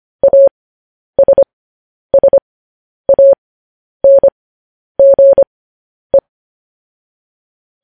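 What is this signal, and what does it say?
Morse code, heard as a keyed tone, about 25 wpm, spaced as for 10 wpm, 550 Hz. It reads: ASSANGE
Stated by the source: Morse code news headlines sent at 10 wpm for practice.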